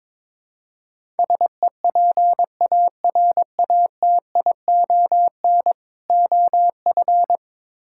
Morse code, heard as a keyed tone, about 21 wpm, 700 Hz. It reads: SEPARATION OF
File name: practice / 22wpm / Build A Phrase - 5-Word Phrases - 22wpm